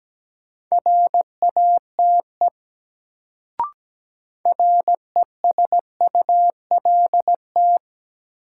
Morse code, heard as a keyed tone, about 17 wpm, 700 Hz. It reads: RATE RESULT
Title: Sentences from Top 700 Words - New Word Drill - 17wpm